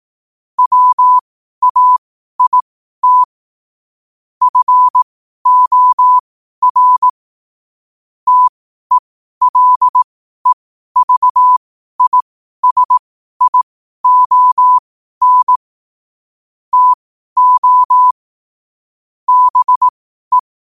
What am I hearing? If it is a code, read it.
Morse code, heard as a keyed tone, about 18 wpm, spaced as for 12 wpm, 1000 Hz. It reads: WAIT FOR TELEVISION TO BE
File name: qotdfiles / 12wpm